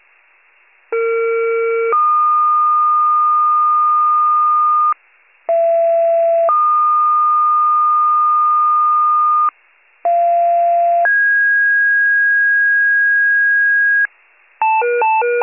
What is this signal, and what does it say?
Two-tone page, heard as a one-second tone, then a three-second tone, about 460 then 1220 Hz, about 670 then 1220 Hz, about 690 then 1650 Hz, with noise. It is heard from 0.9 to 14.1 s.
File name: pager / 41 RESCUE DUTY / wcalert